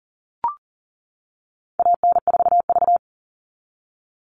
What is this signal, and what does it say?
Morse code, heard as a keyed tone, about 40 wpm, 700 Hz. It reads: AN4V
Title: All Cut Numbers - 4-Digit 40wpm